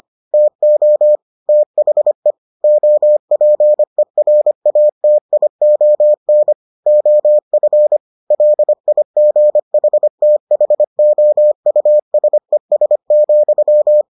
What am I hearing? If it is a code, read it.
Morse code, heard as a keyed tone, about 25 wpm, 600 Hz. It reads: TO THE OPERATION OF LIGHTHOUSES,